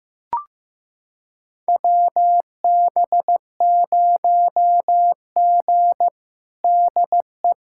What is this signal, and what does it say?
Morse code, heard as a keyed tone, about 16 wpm, 700 Hz. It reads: WB0G DE
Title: USA Callsign DE Callsign - 4 Characters - 15wpm